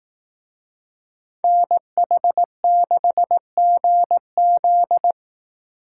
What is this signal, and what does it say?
Morse code, heard as a keyed tone, about 18 wpm, 700 Hz. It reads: NH6GZ
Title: Speed Racing - USA 2x2 Call Signs 15wpm